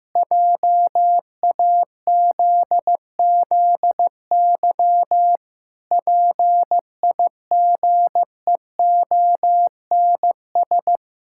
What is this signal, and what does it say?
Morse code, heard as a keyed tone, about 15 wpm, 700 Hz. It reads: JAZZY PIGEONS
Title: Pangram Challenge v2 - 15wpm